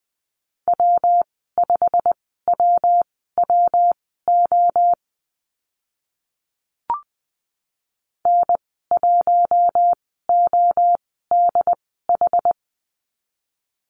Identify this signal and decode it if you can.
Morse code, heard as a keyed tone, about 20 wpm, spaced as for 14 wpm, 700 Hz. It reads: W5WWO N1ODH
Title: 161 Lesson 26g - Letter g - Call Signs 20wpm